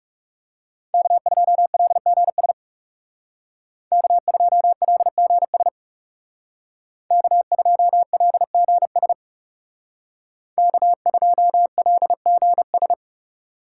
Morse code, heard as a keyed tone, about 39 wpm, 700 Hz. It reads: K2LGS K2LGS K2LGS K2LGS